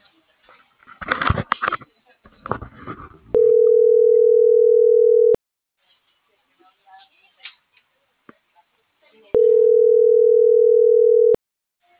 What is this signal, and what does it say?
Ringback tone (North America), 440 + 480 Hz.